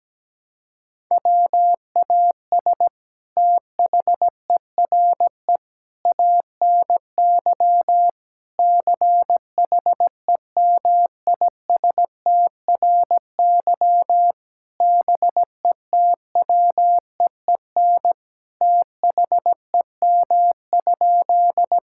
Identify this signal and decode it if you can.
Morse code, heard as a keyed tone, about 17 wpm, 700 Hz. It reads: WAS THERE ANY CHEMISTRY BETWEEN THEM?
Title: Sentences from Top 1000 Words - Review Part 6 - 17wpm